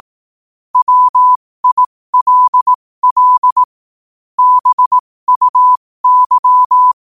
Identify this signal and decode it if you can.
Morse code, heard as a keyed tone, about 18 wpm, spaced as for 15 wpm, 1000 Hz. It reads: WILL BUY
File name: qotdfiles / 15wpm